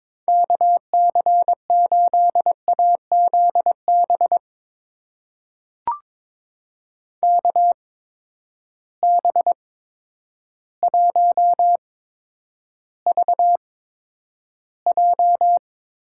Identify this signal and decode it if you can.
Morse code, heard as a keyed tone, about 22 wpm, spaced as for 10 wpm, 700 Hz. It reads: KC8AZB KB1VJ
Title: USA Call Signs - 6 Characters - 8x Character Spacing - 22wpm